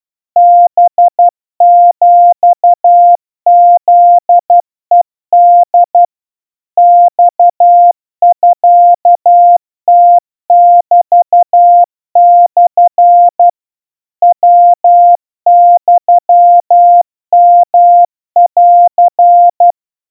Morse code (Japanese Wabun code), heard as a keyed tone, about 12 wpm, 700 Hz. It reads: ハヒフヘホ マミムメモ ヤユヨン
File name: aiueo 40 cpm 700 Hz Crispy